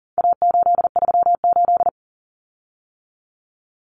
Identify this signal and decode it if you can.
Morse code, heard as a keyed tone, about 40 wpm, 700 Hz. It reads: A838
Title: Common Cut Numbers - 4-Digit 40wpm